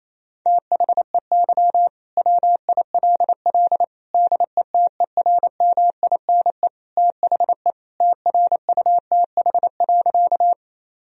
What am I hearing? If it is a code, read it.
Morse code, heard as a keyed tone, about 27 wpm, 700 Hz. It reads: THEY WILL DETERMINE THE TRUTH.
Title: Sentences from Top 1000 Words - Review Part 4 - 28wpm